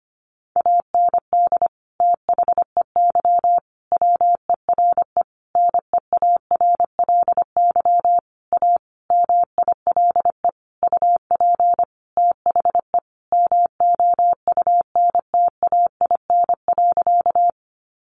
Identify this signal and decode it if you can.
Morse code, heard as a keyed tone, about 25 wpm, 700 Hz. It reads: AND THEY WERE NEARLY A MILE UP THE MOUNTAIN.
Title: Sentences from Top 1300 Words 25wpm